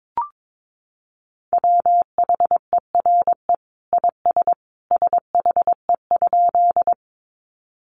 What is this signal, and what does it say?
Morse code, heard as a keyed tone, about 22 wpm, 700 Hz. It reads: WHERE IS SHE?
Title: Sentences from Top 200 Words 22wpm